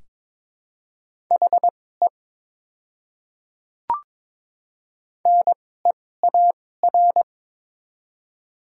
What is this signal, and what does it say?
Morse code, heard as a keyed tone, about 21 wpm, spaced as for 16 wpm, 700 Hz. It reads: HE NEAR
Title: 2x Character Spacing - Top 300 Words - 22wpm